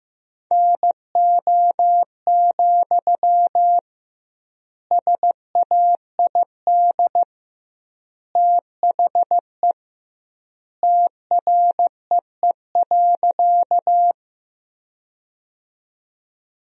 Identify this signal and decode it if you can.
Morse code, heard as a keyed tone, about 16 wpm, 700 Hz. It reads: NO, SAID THE TREE.
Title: 2x Word-Spacing - Sentences from Top 300 Words - 15wpm